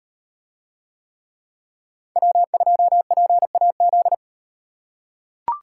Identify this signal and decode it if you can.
Morse code, heard as a keyed tone, about 38 wpm, 700 Hz. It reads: W2PAZ